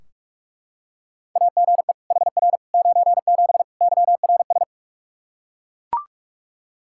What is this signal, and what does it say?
Morse code, heard as a keyed tone, about 41 wpm, 700 Hz. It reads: AGE HR 97 YRS
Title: Practice QSO - Exchange 3 - More Info - 45wpm